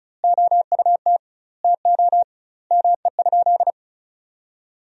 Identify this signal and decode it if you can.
Morse code, heard as a keyed tone, about 34 wpm, 700 Hz. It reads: OUT TO ME?
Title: 2x Word-Spacing - Sentences from Top 200 Words - 35wpm